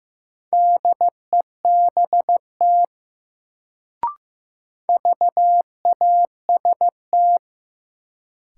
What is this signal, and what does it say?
Morse code, heard as a keyed tone, about 15 wpm, 700 Hz. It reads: DEBT VAST
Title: Sentences from Top 1300 Words - New Word Drill - 15wpm